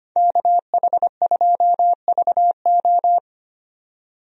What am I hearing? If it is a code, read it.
Morse code, heard as a keyed tone, about 25 wpm, 700 Hz. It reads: KH2VO
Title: USA Call Signs - 5 Characters - 25wpm